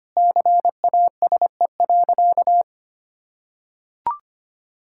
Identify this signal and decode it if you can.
Morse code, heard as a keyed tone, about 25 wpm, 700 Hz. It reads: CASE.
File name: practice / 25wpm / Sentences with 2-Word Phrases - 25wpm